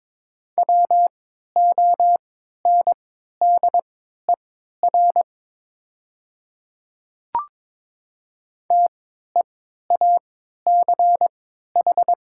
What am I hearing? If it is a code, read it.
Morse code, heard as a keyed tone, about 22 wpm, spaced as for 12 wpm, 700 Hz. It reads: WONDER TEACH
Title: 3x Character Spacing - Top 500 Words - 22wpm